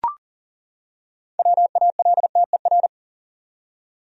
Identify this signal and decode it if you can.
Morse code, heard as a keyed tone, about 40 wpm, 700 Hz. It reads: WALTER
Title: USA Common Names - 40wpm